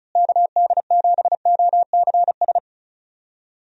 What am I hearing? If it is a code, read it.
Morse code, heard as a keyed tone, about 34 wpm, 700 Hz. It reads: KD7OCS
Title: USA Call Signs - 6 Characters - Repeated 3x - 35wpm